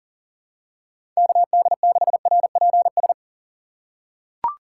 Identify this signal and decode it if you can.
Morse code, heard as a keyed tone, about 37 wpm, 700 Hz. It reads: KD6RPS
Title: USA Call Signs - 6 Characters - 2x Character Spacing - 40wpm